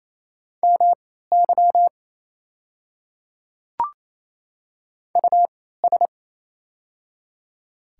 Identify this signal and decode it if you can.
Morse code, heard as a keyed tone, about 28 wpm, spaced as for 16 wpm, 700 Hz. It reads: MY US
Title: 3x Character Spacing - Top 200 Words - 28wpm